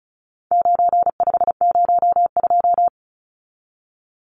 Morse code, heard as a keyed tone, about 34 wpm, 700 Hz. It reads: 9502
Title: Numbers - 4-Digit 35wpm